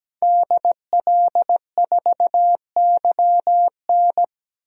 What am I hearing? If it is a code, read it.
Morse code, heard as a keyed tone, about 17 wpm, 700 Hz. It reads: DL4YN
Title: DXCC-100 Call Signs - 5 Characters - Repeated 2x - 17wpm